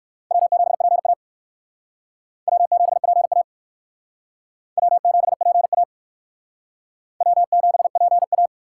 Morse code, heard as a keyed tone, about 65 wpm, 700 Hz. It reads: W7PA W7PA W7PA W7PA